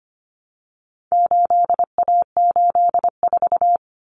Morse code, heard as a keyed tone, about 25 wpm, 700 Hz. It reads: 8A84
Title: Common Cut Numbers - 4-Digit 25wpm